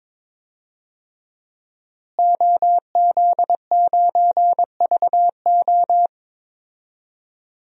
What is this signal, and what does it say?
Morse code, heard as a keyed tone, about 21 wpm, 700 Hz. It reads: OZ9VO
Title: DXCC-100 Call Signs - 5 Characters - Repeated 2x - 22wpm